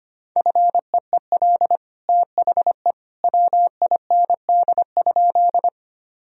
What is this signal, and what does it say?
Morse code, heard as a keyed tone, about 25 wpm, 700 Hz. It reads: FEEL THE WIND?
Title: Sentences from Top 1000 Words - Review Part 2 - 25wpm